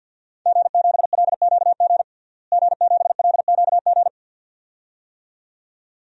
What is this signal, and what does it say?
Morse code, heard as a keyed tone, about 52 wpm, 700 Hz. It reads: G7LQG G7LQG
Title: DXCC-100 Call Signs - 5 Characters - Repeated 2x - 50wpm